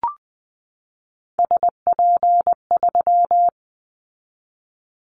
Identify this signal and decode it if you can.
Morse code, heard as a keyed tone, about 20 wpm, 700 Hz. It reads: SP3
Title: Callsign Prefixes for Top 25 DXCC - 20wpm